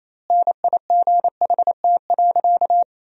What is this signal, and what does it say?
Morse code, heard as a keyed tone, about 27 wpm, 700 Hz. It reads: NIGHT.
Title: 2x Word-Spacing - Sentences from Top 300 Words - 28wpm